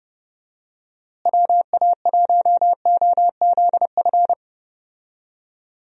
Morse code, heard as a keyed tone, about 30 wpm, 700 Hz. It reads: WA1OZF
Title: USA Call Signs - 6 Characters - Repeated 3x - 30wpm